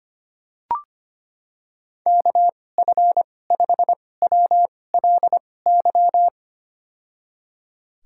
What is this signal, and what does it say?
Morse code, heard as a keyed tone, about 25 wpm, spaced as for 18 wpm, 700 Hz. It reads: KF5WLY
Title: USA Call Signs - 6 Characters - 2x Character Spacing - 25wpm